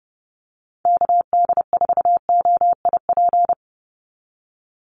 Morse code, heard as a keyed tone, about 30 wpm, 700 Hz. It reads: KD4OIP
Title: USA Call Signs 30wpm